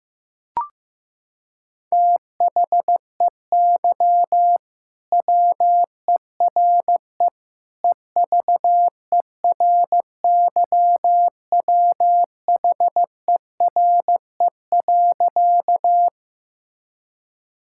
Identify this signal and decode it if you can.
Morse code, heard as a keyed tone, about 15 wpm, 700 Hz. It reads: THEY WERE EVERYWHERE.
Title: Sentences from Top 1600 Words 15wpm